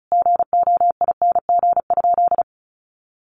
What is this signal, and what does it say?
Morse code, heard as a keyed tone, about 34 wpm, 700 Hz. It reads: GOING?
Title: Sentences from Top 1300 Words 35wpm